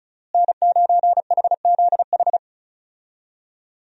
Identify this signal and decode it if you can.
Morse code, heard as a keyed tone, about 36 wpm, 700 Hz. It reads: N9HZH
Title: USA Call Signs - 5 Characters - Repeated 3x - 35wpm